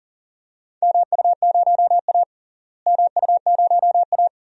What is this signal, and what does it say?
Morse code, heard as a keyed tone, about 40 wpm, 700 Hz. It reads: MU0A MU0A
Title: DXCC-100 Call Signs - 4 Characters - Repeated 3x - 40wpm